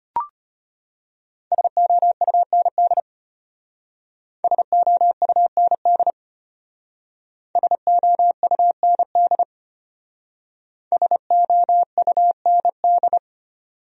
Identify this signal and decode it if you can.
Morse code, heard as a keyed tone, about 32 wpm, 700 Hz. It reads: SOUND SOUND SOUND SOUND